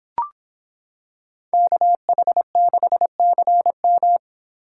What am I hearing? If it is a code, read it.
Morse code, heard as a keyed tone, about 26 wpm, 700 Hz. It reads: KH6CM